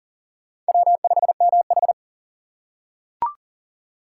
Morse code, heard as a keyed tone, about 37 wpm, 700 Hz. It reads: W5MH
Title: USA Call Signs - 4 Characters - 6x Character Spacing - 40wpm